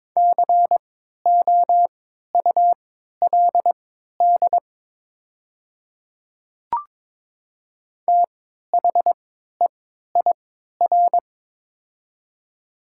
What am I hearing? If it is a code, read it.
Morse code, heard as a keyed tone, about 22 wpm, spaced as for 13 wpm, 700 Hz. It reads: COULD THEIR
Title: 3x Character Spacing - Top 200 Words - 22wpm